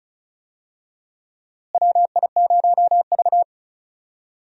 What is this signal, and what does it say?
Morse code, heard as a keyed tone, about 36 wpm, 700 Hz. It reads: WI0V
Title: USA Call Signs - 4 Characters - Repeated 2x - 35wpm